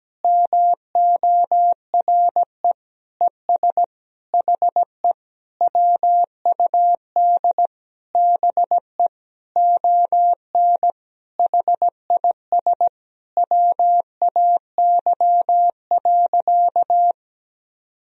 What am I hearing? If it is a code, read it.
Morse code, heard as a keyed tone, about 17 wpm, 700 Hz. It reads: MORE ES HE WUD BE ON HIS WAY.
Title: Sentences from Top 300 Words and Abbreviations 17wpm